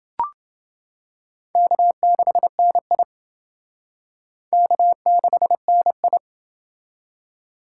Morse code, heard as a keyed tone, about 29 wpm, 700 Hz. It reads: K6NI K6NI